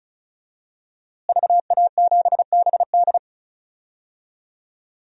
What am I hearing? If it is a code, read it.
Morse code, heard as a keyed tone, about 34 wpm, 700 Hz. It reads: VA7BD
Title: DXCC-100 Call Signs - 5 Characters - Repeated 2x - 35wpm